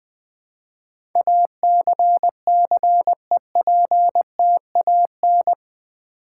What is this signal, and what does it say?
Morse code, heard as a keyed tone, about 20 wpm, 700 Hz. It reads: ACCEPTAN
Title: Build A Word - 4 Letters At A Time - 10-Letter Words - 20wpm